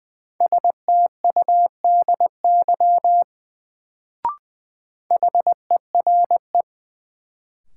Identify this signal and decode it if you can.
Morse code, heard as a keyed tone, about 21 wpm, 700 Hz. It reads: STUDY HERE
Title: Sets of 5 Words - New Word Drill - 20wpm